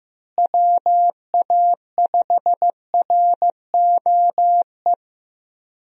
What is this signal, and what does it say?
Morse code, heard as a keyed tone, about 15 wpm, 700 Hz. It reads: WA5ROE
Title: USA Call Signs - 6 Characters - Repeated 3x - 15wpm